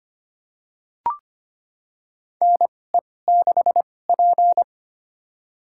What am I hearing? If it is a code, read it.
Morse code, heard as a keyed tone, about 25 wpm, spaced as for 18 wpm, 700 Hz. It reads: NE6P
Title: USA Call Signs - 4 Characters - 2x Character Spacing - 25wpm